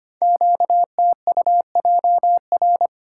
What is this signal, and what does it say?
Morse code, heard as a keyed tone, about 25 wpm, 700 Hz. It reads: QTUJR
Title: Code Groups - 5 Letters - 25wpm